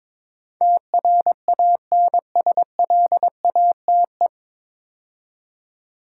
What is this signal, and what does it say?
Morse code, heard as a keyed tone, about 22 wpm, 690 Hz. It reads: TRANSLATE
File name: 9 Letter Words - First 3 Letters Given - 22wpm